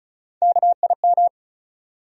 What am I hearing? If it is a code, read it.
Morse code, heard as a keyed tone, about 35 wpm, 700 Hz. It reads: KIM
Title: USA Common Names - Max 4 Letters - 35wpm